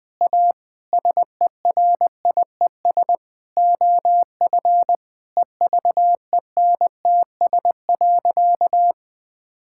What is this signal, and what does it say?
Morse code, heard as a keyed tone, about 21 wpm, 700 Hz. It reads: A SERIES OF EVENTS.
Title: Sentences with 6-Word Phrases - 20wpm